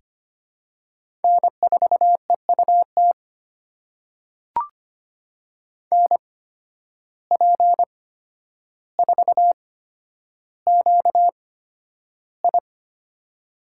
Morse code, heard as a keyed tone, about 25 wpm, spaced as for 11 wpm, 700 Hz. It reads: N4EUT NP4QI
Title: USA Call Signs - 5 Characters - 8x Character Spacing - 25wpm